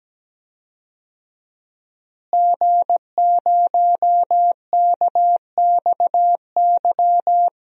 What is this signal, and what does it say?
Morse code, heard as a keyed tone, about 17 wpm, 700 Hz. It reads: G0KXY
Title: DXCC-100 Call Signs - 5 Characters - Repeated 2x - 17wpm